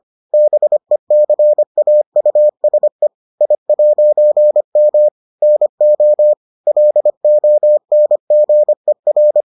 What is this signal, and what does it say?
Morse code, heard as a keyed tone, about 25 wpm, 600 Hz. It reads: BECAUSE I'M NO LONGER